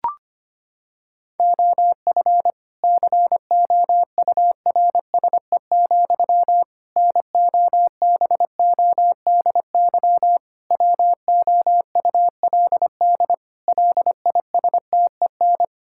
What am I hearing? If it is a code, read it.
Morse code, heard as a keyed tone, about 25 wpm, 700 Hz. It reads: OF COURSE, NOBODY WOULD LISTEN
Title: Sentences from Top 1600 Words 25wpm